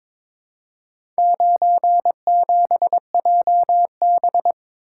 Morse code, heard as a keyed tone, about 22 wpm, 700 Hz. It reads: 97JB
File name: practice / 22wpm / Four Letter-Number Combinations - 22wpm